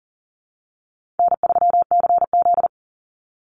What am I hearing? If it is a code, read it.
Morse code, heard as a keyed tone, about 40 wpm, 700 Hz. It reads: N3CZ
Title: Easy Call Signs - 40wpm